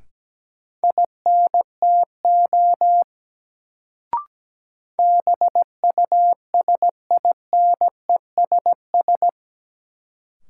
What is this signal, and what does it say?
Morse code, heard as a keyed tone, about 17 wpm, 700 Hz. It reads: INTO BUSINESS
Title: Sentences from Top 600 Words - New Word Drill - 17wpm